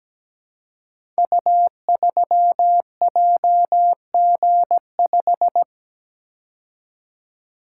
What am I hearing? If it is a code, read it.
Morse code, heard as a keyed tone, about 17 wpm, 700 Hz. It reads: U3JG5
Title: Code Groups - 5 Letter:Numbers - 17wpm